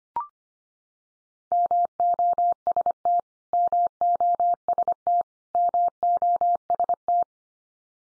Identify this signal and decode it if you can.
Morse code, heard as a keyed tone, about 25 wpm, 700 Hz. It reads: MOST MOST MOST
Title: Top 100 Words 25wpm x3